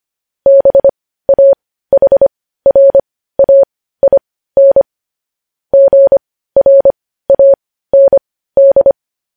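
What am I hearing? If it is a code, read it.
Morse code, heard as a keyed tone, about 25 wpm, spaced as for 15 wpm, 550 Hz. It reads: BAHRAIN GRAND
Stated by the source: Morse news headlines practice sent at 15 WPM.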